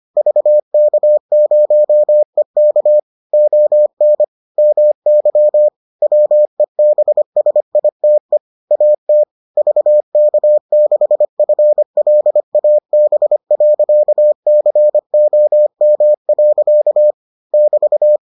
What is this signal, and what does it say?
Morse code, heard as a keyed tone, about 25 wpm, 600 Hz. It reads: VK0EK ON MY WEBSITE AT VK6FLAB.COM. <BT>